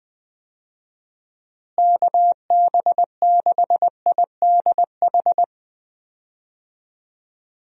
Morse code, heard as a keyed tone, about 21 wpm, 700 Hz. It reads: KB6IDH
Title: USA Call Signs - 6 Characters - Repeated 2x - 20wpm